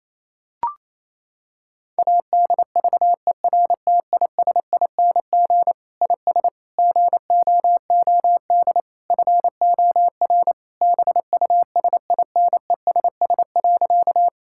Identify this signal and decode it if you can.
Morse code, heard as a keyed tone, about 28 wpm, 700 Hz. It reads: ADVERTISING IS GOOD FOR BUSINESS.
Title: Sentences from Top 1000 Words - Review Part 6 - 28wpm